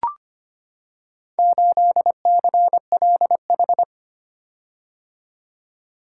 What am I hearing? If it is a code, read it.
Morse code, heard as a keyed tone, about 25 wpm, 700 Hz. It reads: 8CLH
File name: practice / 25wpm / Four Letter-Number Combinations - 25wpm